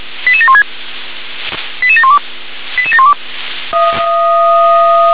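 Five-tone selective call (ZVEI), heard as quick runs of five tones, tones of about 70 ms. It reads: 88716 88712 88712
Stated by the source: ZVEI five-tone selective call received by SDR.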